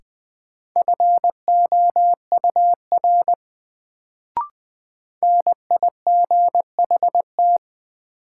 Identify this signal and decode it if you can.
Morse code, heard as a keyed tone, about 20 wpm, 700 Hz. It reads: FOUR NIGHT
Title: Sentences from Top 300 Words - New Word Drill - 20wpm